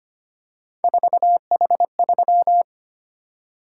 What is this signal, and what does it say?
Morse code, heard as a keyed tone, about 25 wpm, 700 Hz. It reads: 4H3
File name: Three Letter-Number Combinations - 25wpm